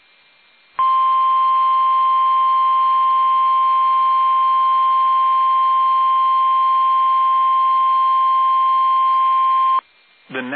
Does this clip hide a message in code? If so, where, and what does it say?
Yes: Emergency Alert System (SAME) tones (the 1050 Hz attention tone) from 0.7 to 9.8 s, with speech.